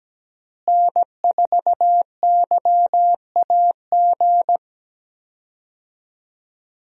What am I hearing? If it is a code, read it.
Morse code, heard as a keyed tone, about 17 wpm, 700 Hz. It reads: N4YAG
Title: USA Call Signs - 5 Characters - Repeated 3x - 17wpm